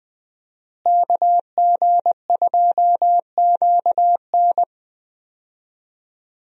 Morse code, heard as a keyed tone, about 21 wpm, 700 Hz. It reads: KG2QN
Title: USA Call Signs - 5 Characters - Repeated 2x - 20wpm